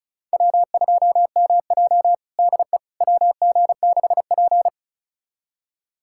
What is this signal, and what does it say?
Morse code, heard as a keyed tone, about 34 wpm, 700 Hz. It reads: W2MJ DE WG6P
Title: USA Callsign DE Callsign - 4 Characters - 35wpm